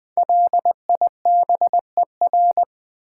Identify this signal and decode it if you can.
Morse code, heard as a keyed tone, about 20 wpm, 700 Hz. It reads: LIBER
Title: Build A Word - 5 Letters At A Time - 10-Letter Words - 20wpm